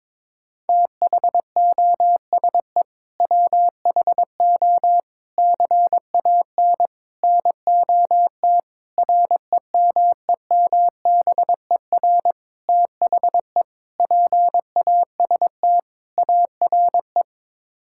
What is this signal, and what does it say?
Morse code, heard as a keyed tone, about 22 wpm, 700 Hz. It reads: THOSE WHO CAN NOT REMEMBER THE PAST ARE
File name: practice / 22wpm / Sets of 8 Words - 22wpm